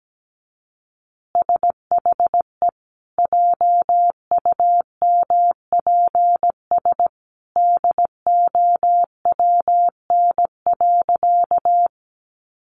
Morse code, heard as a keyed tone, about 17 wpm, 700 Hz. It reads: SHE JUMPS DOWN.
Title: Easy Phrases 17wpm